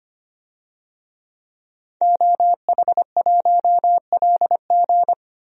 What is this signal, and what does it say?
Morse code, heard as a keyed tone, about 25 wpm, 700 Hz. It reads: OH1LG
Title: DXCC-100 Call Signs - 5 Characters - 25wpm